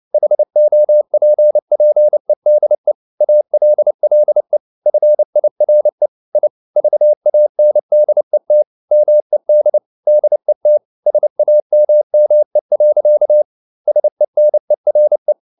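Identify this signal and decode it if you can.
Morse code, heard as a keyed tone, about 29 wpm, 600 Hz. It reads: HOPPEDE ALLE FIRE I VANDET MED DET SAMME. SENERE
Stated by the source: Fast (QRQ) Morse at 29 wpm, from CW med Gnister.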